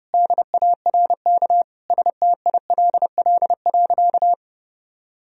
Morse code, heard as a keyed tone, about 30 wpm, 700 Hz. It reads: DARK STILL.
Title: Sentences from Top 1000 Words - Review Part 4 - 30wpm